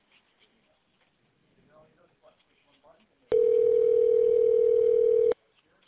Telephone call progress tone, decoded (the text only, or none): ringback tone